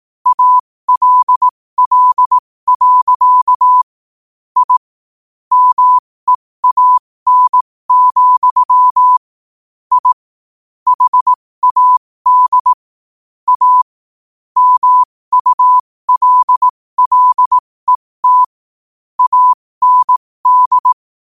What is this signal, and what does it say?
Morse code, heard as a keyed tone, about 18 wpm, spaced as for 15 wpm, 1000 Hz. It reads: ALL. I MEAN, I HAD A MULLET AND
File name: qotdfiles / 15wpm